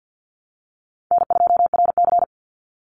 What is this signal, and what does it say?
Morse code, heard as a keyed tone, about 50 wpm, 700 Hz. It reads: N2RC
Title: Easy Call Signs Repeated 2x - 50wpm